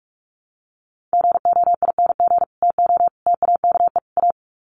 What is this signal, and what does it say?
Morse code, heard as a keyed tone, about 44 wpm, 700 Hz. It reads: GOING TO TAKE A
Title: Sets of 4 Words 45wpm